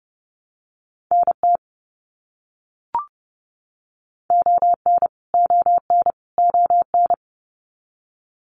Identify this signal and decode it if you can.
Morse code, heard as a keyed tone, about 30 wpm, 700 Hz. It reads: NT ON ON ON
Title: Canadian Provinces and Territories - Repeated 3x - 30wpm